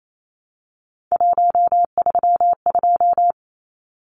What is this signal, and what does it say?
Morse code, heard as a keyed tone, about 28 wpm, 700 Hz. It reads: 132
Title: Numbers - 3-Digit 28wpm